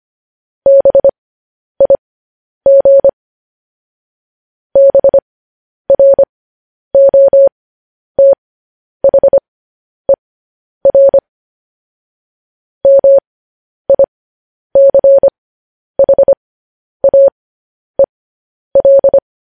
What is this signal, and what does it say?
Morse code, heard as a keyed tone, about 25 wpm, spaced as for 10 wpm, 550 Hz. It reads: BIG BROTHER MICHAEL
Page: news headlines in morse code at 10 wpm